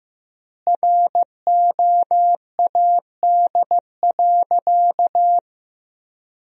Morse code, heard as a keyed tone, about 15 wpm, 700 Hz. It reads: ROAD.
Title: 3x Word-Spacing - Sentences from Top 400 Words - 15wpm